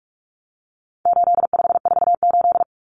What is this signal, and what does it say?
Morse code, heard as a keyed tone, about 44 wpm, 700 Hz. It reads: 8548
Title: Numbers - 4-Digit 45wpm